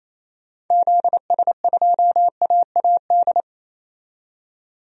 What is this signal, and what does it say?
Morse code, heard as a keyed tone, about 28 wpm, 700 Hz. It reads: ZS2AAD
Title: DXCC-100 Call Signs - 6 Characters - 28wpm